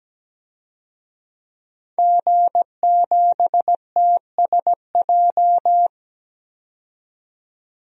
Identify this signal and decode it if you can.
Morse code, heard as a keyed tone, about 17 wpm, 700 Hz. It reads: G7TSJ
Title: DXCC-100 Call Signs - 5 Characters - 17wpm